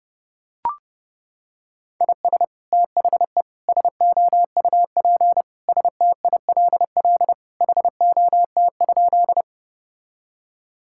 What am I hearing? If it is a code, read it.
Morse code, heard as a keyed tone, about 30 wpm, 700 Hz. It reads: IS THE SOUP STILL HOT?
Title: Sentences from Top 1000 Words - Review Part 6 - 30wpm